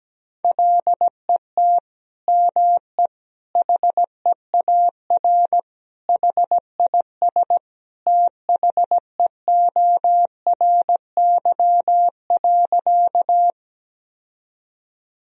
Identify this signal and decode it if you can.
Morse code, heard as a keyed tone, about 17 wpm, 700 Hz. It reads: LET ME HEAR HIS THEORY.
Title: Sentences from Top 1000 Words - Review Part 3 - 17wpm